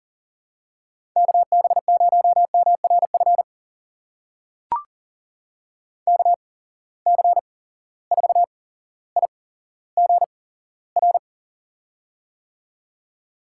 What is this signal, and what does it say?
Morse code, heard as a keyed tone, about 40 wpm, spaced as for 17 wpm, 700 Hz. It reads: KB0MRF KC4IGR
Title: USA Call Signs - 6 Characters - 8x Character Spacing - 40wpm